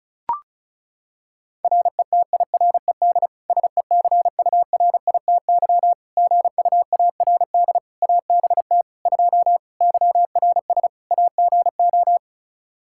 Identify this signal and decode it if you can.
Morse code, heard as a keyed tone, about 36 wpm, 700 Hz. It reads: RETIRED SECURITY GUARD ABT 2 YRS AGO
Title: Practice QSO - Exchange 3 - More Info - 35wpm